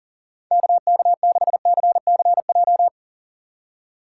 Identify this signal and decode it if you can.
Morse code, heard as a keyed tone, about 40 wpm, 700 Hz. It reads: KK6CCJ
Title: USA Call Signs - 6 Characters - 3x Character Spacing - 40wpm